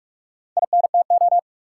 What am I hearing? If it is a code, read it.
Morse code, heard as a keyed tone, about 46 wpm, 700 Hz. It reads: INTO
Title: Sentences from Top 600 Words - New Word Drill - 45wpm